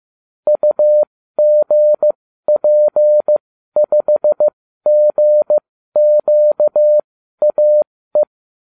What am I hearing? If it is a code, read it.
Morse code, heard as a keyed tone, about 15 wpm, spaced as for 12 wpm, 600 Hz. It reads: UGP5GQAE